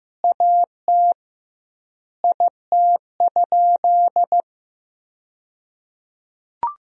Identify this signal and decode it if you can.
Morse code, heard as a keyed tone, about 16 wpm, 700 Hz. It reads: AT IT?
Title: 2x Word-Spacing - Sentences from Top 100 Words - 15wpm